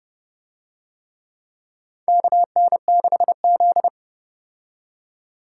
Morse code, heard as a keyed tone, about 29 wpm, 700 Hz. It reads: KN6Z